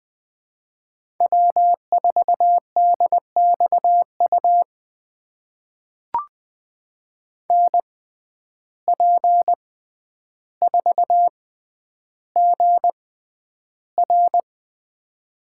Morse code, heard as a keyed tone, about 20 wpm, spaced as for 10 wpm, 700 Hz. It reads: W4DXU NP4GR
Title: USA Call Signs - 5 Characters - 6x Character Spacing - 20wpm